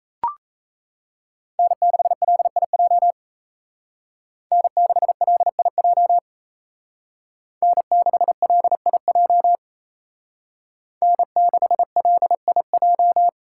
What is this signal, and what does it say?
Morse code, heard as a keyed tone, about 36 wpm, 700 Hz. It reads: N6LIJ N6LIJ N6LIJ N6LIJ